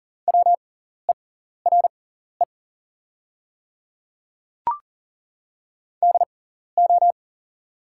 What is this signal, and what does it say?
Morse code, heard as a keyed tone, about 38 wpm, spaced as for 14 wpm, 690 Hz. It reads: WERE DO
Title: 6x Character Spacing - Top 100 Words - 40wpm